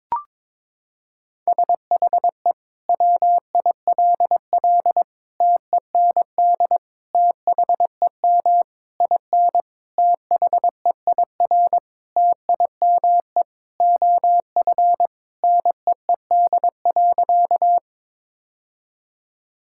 Morse code, heard as a keyed tone, about 22 wpm, 700 Hz. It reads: SHE WILL TEND THEM IN THEIR TIME OF NEED.
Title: Sentences from Top 1000 Words - Review Part 5 - 22wpm